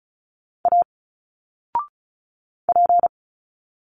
Morse code, heard as a keyed tone, about 34 wpm, 700 Hz. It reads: A P
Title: Single Letters 35wpm